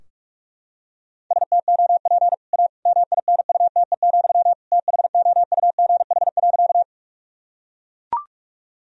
Morse code, heard as a keyed tone, about 43 wpm, 700 Hz. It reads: STOP A MINUTE, THOUGH.
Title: Sentences from Top 1000 Words - Review Part 2 - 45wpm